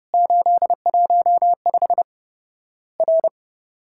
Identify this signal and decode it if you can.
Morse code, heard as a keyed tone, about 31 wpm, 700 Hz. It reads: 815 R
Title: Worked All Europe DX Contest - QTC Exchange - 30wpm